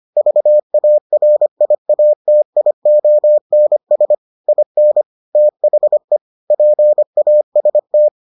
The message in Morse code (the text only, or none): VARIATIONS IN THE PAST